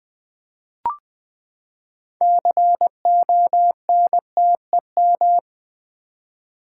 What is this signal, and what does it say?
Morse code, heard as a keyed tone, about 20 wpm, 700 Hz. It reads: CONTEM
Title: Build A Word - 6 Letters At A Time - 12-Letter Words - 20wpm